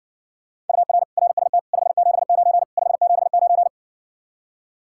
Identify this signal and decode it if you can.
Morse code, heard as a keyed tone, about 60 wpm, 700 Hz. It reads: UR RST 579 579